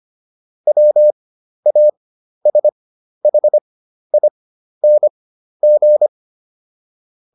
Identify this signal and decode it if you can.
Morse code, heard as a keyed tone, about 25 wpm, spaced as for 12 wpm, 600 Hz. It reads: WASHING